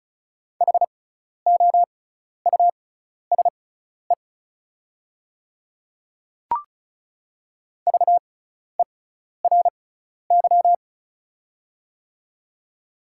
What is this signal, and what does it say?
Morse code, heard as a keyed tone, about 34 wpm, spaced as for 12 wpm, 690 Hz. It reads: HOUSE VERY
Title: 6x Character Spacing - Top 300 Words - 35wpm